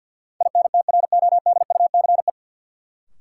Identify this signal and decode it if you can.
Morse code, heard as a keyed tone, about 53 wpm, 700 Hz. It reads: INTRODUCE
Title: Sentences from Top 1600 Words - New Word Drill Alt - 50wpm